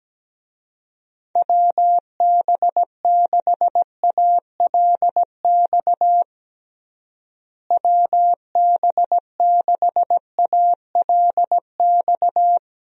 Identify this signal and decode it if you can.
Morse code, heard as a keyed tone, about 17 wpm, 700 Hz. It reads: WB6ALX WB6ALX
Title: USA Call Signs - 6 Characters - Repeated 3x - 17wpm